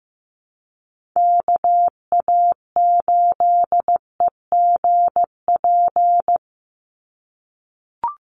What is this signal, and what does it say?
Morse code, heard as a keyed tone, about 15 wpm, 700 Hz. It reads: KA8EGP
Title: USA Call Signs 15wpm